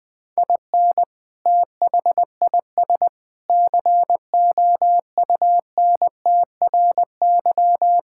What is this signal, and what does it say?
Morse code, heard as a keyed tone, about 20 wpm, 700 Hz. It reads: IN THIS COUNTRY
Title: Sentences with 3-Word Phrases - 20wpm